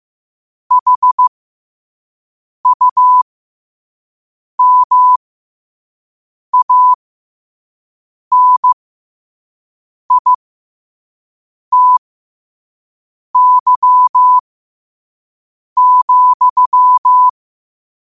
Morse code, heard as a keyed tone, about 15 wpm, spaced as for 5 wpm, 1000 Hz. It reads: HUMANITY,
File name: qotdfiles / 5wpm